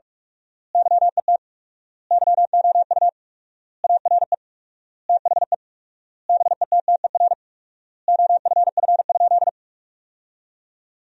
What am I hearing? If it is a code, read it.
Morse code, heard as a keyed tone, about 46 wpm, 700 Hz. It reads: YET YOU ARE THE BETTER OFF?
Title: 4x Word-Spacing - Sentences from Top 500 Words - 45wpm